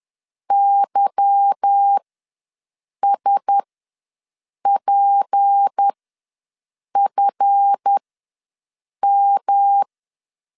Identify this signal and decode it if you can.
Morse code, heard as a keyed tone, about 10 wpm, spaced as for 6 wpm, 800 Hz. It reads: YSPFM